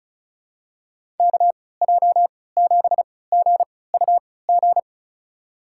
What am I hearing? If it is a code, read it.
Morse code, heard as a keyed tone, about 36 wpm, spaced as for 20 wpm, 700 Hz. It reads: KJ7GUG
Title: USA Call Signs - 6 Characters - 3x Character Spacing - 35wpm